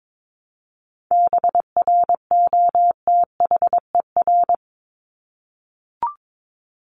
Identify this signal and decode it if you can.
Morse code, heard as a keyed tone, about 22 wpm, 700 Hz. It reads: BROTHER
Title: Top 1751-2000 Words 22wpm